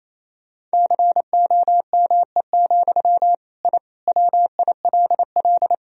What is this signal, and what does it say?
Morse code, heard as a keyed tone, about 27 wpm, 700 Hz. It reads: COME, I WILL
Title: Sentences from Top 1000 Words - Review Part 2 - 28wpm